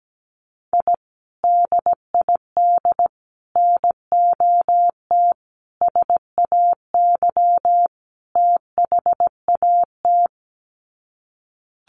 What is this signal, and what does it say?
Morse code, heard as a keyed tone, about 17 wpm, 700 Hz. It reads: I DID NOT SAY THAT
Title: Sets of 5 Words Encore2 17wpm